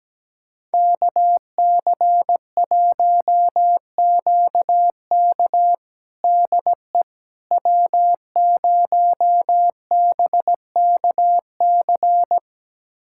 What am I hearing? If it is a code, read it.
Morse code, heard as a keyed tone, about 17 wpm, 700 Hz. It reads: KC1QK DE W0BKC
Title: USA Callsign DE Callsign - 5 Characters - 17wpm